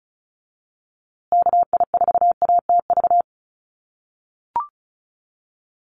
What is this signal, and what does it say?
Morse code, heard as a keyed tone, about 34 wpm, 700 Hz. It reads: KI4ATV
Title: USA Call Signs Repeated 3x 35wpm